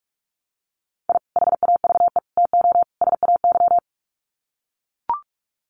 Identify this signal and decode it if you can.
Morse code, heard as a keyed tone, about 44 wpm, 700 Hz. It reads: I HAVE TO SAY
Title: Sets of 4 Words 45wpm